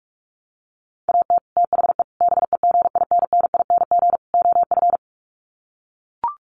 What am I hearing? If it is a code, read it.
Morse code, heard as a keyed tone, about 44 wpm, 700 Hz. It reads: AT THE BEGINNING OF
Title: Sets of 4 Words 45wpm